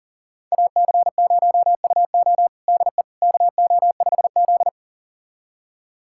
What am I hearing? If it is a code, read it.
Morse code, heard as a keyed tone, about 40 wpm, 700 Hz. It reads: AC0UO DE KO5Z
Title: USA Callsign DE Callsign 40wpm